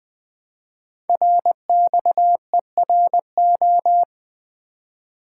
Morse code, heard as a keyed tone, about 21 wpm, 700 Hz. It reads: RXERO
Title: Code Groups - 5 Letters - 20wpm